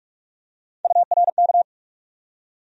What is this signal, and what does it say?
Morse code, heard as a keyed tone, about 46 wpm, 700 Hz. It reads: URK
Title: USA Call Sign Postfixes - 3 Letters 45wpm